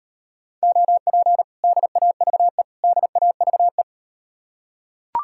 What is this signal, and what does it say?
Morse code, heard as a keyed tone, about 38 wpm, 700 Hz. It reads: OP DAVE DAVE